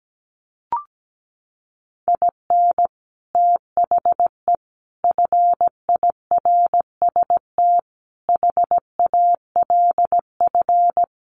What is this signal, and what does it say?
Morse code, heard as a keyed tone, about 17 wpm, 700 Hz. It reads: IN THE FIRST HALF
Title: Sets of 5 Words 17wpm